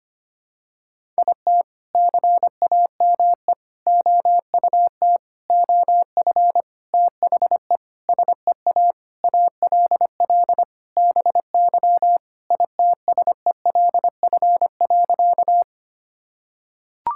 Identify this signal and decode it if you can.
Morse code, heard as a keyed tone, about 25 wpm, 700 Hz. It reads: IT CAME OUT OF THE SEA ALL BY ITSELF.
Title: Sentences from Top 1000 Words - Review Part 5 - 25wpm